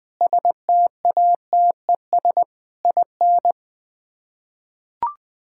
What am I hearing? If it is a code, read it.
Morse code, heard as a keyed tone, about 20 wpm, 700 Hz. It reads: STATES IN
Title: Build A Phrase - 5-Word Phrases - 20wpm